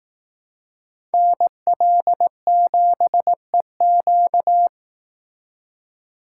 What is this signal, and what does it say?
Morse code, heard as a keyed tone, about 18 wpm, 700 Hz. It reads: NL7EQ
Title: Speed Racing - USA 2x2 Call Signs 15wpm